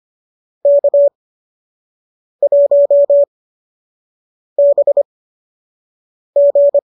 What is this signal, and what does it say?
Morse code, heard as a keyed tone, about 25 wpm, spaced as for 6 wpm, 590 Hz. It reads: K1BG